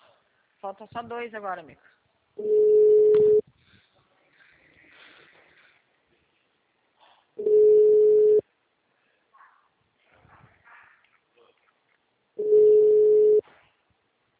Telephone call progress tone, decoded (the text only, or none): ringback tone